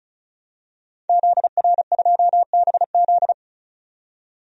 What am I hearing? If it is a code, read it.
Morse code, heard as a keyed tone, about 34 wpm, 700 Hz. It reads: ZR2BZ
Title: DXCC-100 Call Signs - 5 Characters - 35wpm